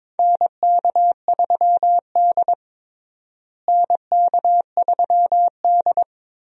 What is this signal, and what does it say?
Morse code, heard as a keyed tone, about 21 wpm, 700 Hz. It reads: NK3D NK3D
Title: USA Call Signs - 4 Characters - Repeated 2x - 22wpm